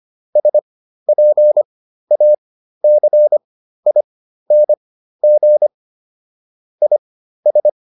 Morse code, heard as a keyed tone, about 25 wpm, spaced as for 13 wpm, 600 Hz. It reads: SPACING IS